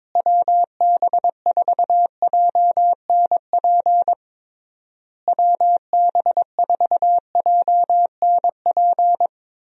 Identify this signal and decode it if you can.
Morse code, heard as a keyed tone, about 21 wpm, 700 Hz. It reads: WB4JNP WB4JNP
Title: USA Call Signs - 6 Characters - Repeated 3x - 22wpm